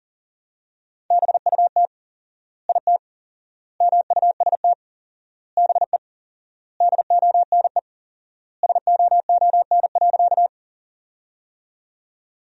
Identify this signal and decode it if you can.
Morse code, heard as a keyed tone, about 37 wpm, 700 Hz. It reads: BUT IT MUST BE DONE SOON.
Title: 4x Word-Spacing - Sentences from Top 500 Words - 40wpm